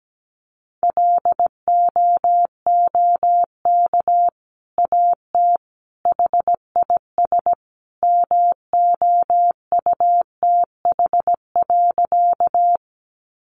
Morse code, heard as a keyed tone, about 17 wpm, 700 Hz. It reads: LOOK AT HIS MOUTH.
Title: Sentences from Top 1500 Words 17wpm